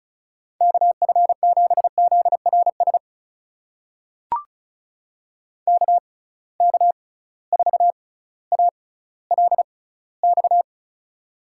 Morse code, heard as a keyed tone, about 36 wpm, spaced as for 18 wpm, 700 Hz. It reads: KF7ZRS KK4ALX
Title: USA Call Signs - 6 Characters - 6x Character Spacing - 35wpm